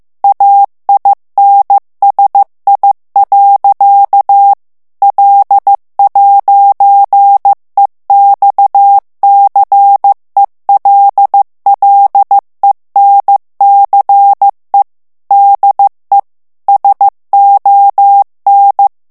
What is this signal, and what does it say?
Morse code, heard as a keyed tone, about 15 wpm, 800 Hz. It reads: AINSI. L'EXCELLENCE DE SON
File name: FRANCAIS TEXTE CLAIR 2 VITESSE 14 MOTS MINUTES